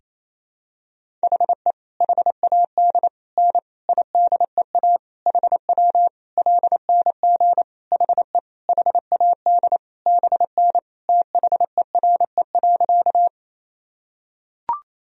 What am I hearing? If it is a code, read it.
Morse code, heard as a keyed tone, about 28 wpm, 700 Hz. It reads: HE HAD N IDEA HW LNG HE HAD BN THERE.
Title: Sentences from Top 300 Words and Abbreviations 28wpm